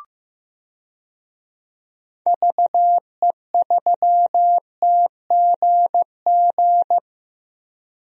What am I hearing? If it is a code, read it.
Morse code, heard as a keyed tone, about 15 wpm, 700 Hz. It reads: VE3TGG
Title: DXCC-100 Call Signs - 6 Characters - Repeated 3x - 15wpm